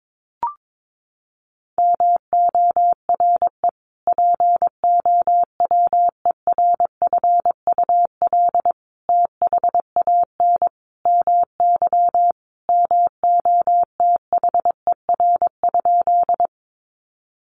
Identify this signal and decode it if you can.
Morse code, heard as a keyed tone, about 22 wpm, 700 Hz. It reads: MORE POWERFUL THAN MY MOTHER?
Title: Sentences from Top 1300 Words 22wpm